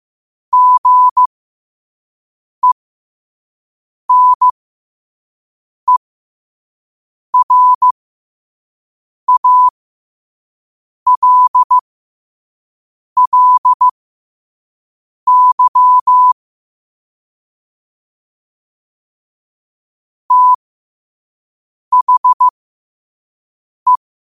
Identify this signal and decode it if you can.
Morse code, heard as a keyed tone, about 15 wpm, spaced as for 5 wpm, 1000 Hz. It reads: GENERALLY THE
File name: qotdfiles / 5wpm